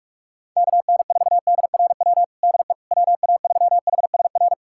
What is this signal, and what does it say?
Morse code, heard as a keyed tone, about 46 wpm, 700 Hz. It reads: KN4DRW DE WA3HSR
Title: USA Callsign DE Callsign 45wpm